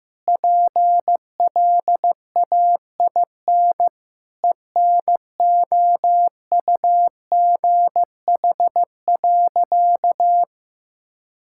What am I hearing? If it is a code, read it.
Morse code, heard as a keyed tone, about 16 wpm, 700 Hz. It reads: PLAIN ENOUGH.
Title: Sentences from Top 1000 Words - Review Part 5 - 15wpm